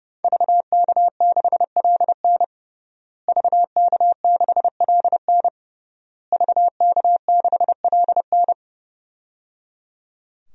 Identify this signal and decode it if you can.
Morse code, heard as a keyed tone, about 30 wpm, 700 Hz. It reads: VK6LN VK6LN VK6LN
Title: DXCC-100 Call Signs - 5 Characters - Repeated 3x - 30wpm